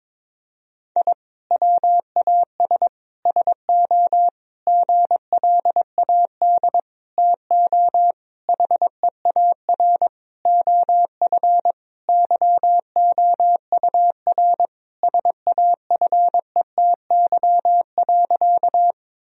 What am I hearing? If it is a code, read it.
Morse code, heard as a keyed tone, about 22 wpm, 700 Hz. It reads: I WAS SO GLAD TO HEAR OF YOUR SAFETY.
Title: Sentences from Top 1000 Words - Review Part 4 - 22wpm